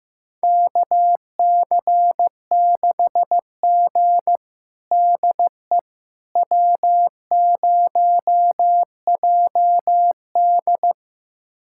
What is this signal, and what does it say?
Morse code, heard as a keyed tone, about 15 wpm, 700 Hz. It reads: KC6G DE W0JD
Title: USA Callsign DE Callsign - 4 Characters - 15wpm